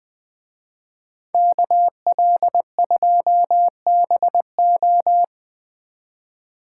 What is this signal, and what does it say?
Morse code, heard as a keyed tone, about 20 wpm, 700 Hz. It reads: KL2BO